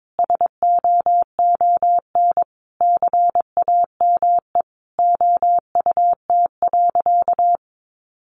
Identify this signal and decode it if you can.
Morse code, heard as a keyed tone, about 22 wpm, 700 Hz. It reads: SOON CAME OUT.
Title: Sentences from Top 1200 Words 22wpm